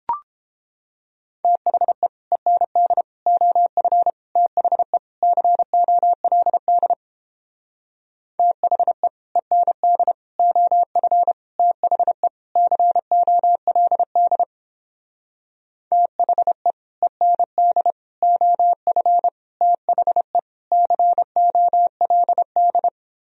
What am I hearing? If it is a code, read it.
Morse code, heard as a keyed tone, about 29 wpm, 700 Hz. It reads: THE END OF THE COLD THE END OF THE COLD THE END OF THE COLD